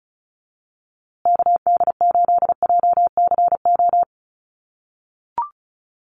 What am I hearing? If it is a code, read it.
Morse code, heard as a keyed tone, about 34 wpm, 700 Hz. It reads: KD8JCO
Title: USA Call Signs Repeated 3x 35wpm